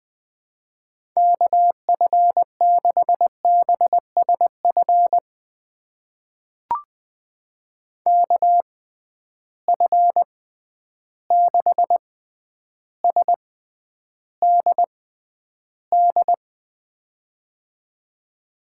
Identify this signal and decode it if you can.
Morse code, heard as a keyed tone, about 20 wpm, spaced as for 10 wpm, 700 Hz. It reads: KF6BSF KF6SDD
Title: USA Call Signs - 6 Characters - 6x Character Spacing - 20wpm